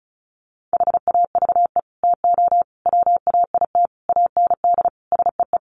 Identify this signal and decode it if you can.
Morse code, heard as a keyed tone, about 34 wpm, 700 Hz. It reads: HAVE TO WAIT AND SEE
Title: Sets of 5 Words Encore2 35wpm